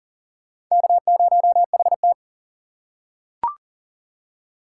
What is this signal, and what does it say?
Morse code, heard as a keyed tone, about 40 wpm, 700 Hz. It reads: K0HT
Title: K1USN SST QSO Practice - 40wpm